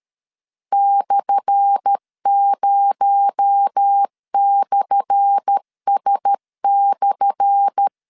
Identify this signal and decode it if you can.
Morse code, heard as a keyed tone, about 13 wpm, 800 Hz. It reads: /0/S/
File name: LEZIONI / 10 SIGNS / 10A SLASH 12wpm 2314